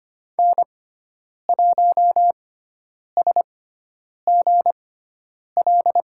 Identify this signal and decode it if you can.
Morse code, heard as a keyed tone, about 25 wpm, spaced as for 9 wpm, 700 Hz. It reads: N1SGL
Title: USA Call Signs - 5 Characters - 6x Character Spacing - 25wpm